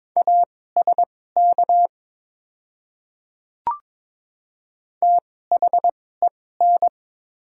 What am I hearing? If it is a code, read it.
Morse code, heard as a keyed tone, about 22 wpm, spaced as for 16 wpm, 700 Hz. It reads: ASK THEN
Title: 2x Character Spacing - Top 200 Words - 22wpm